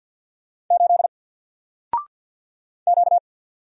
Morse code, heard as a keyed tone, about 48 wpm, 700 Hz. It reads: 8 Q
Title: Single Character (Letter:Number:Punctuation:Prosign) 50wpm